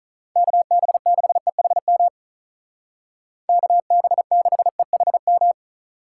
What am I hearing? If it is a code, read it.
Morse code, heard as a keyed tone, about 44 wpm, 700 Hz. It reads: KB6EHM KB6EHM
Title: Speed Racing - USA 2x3 Call Signs 35wpm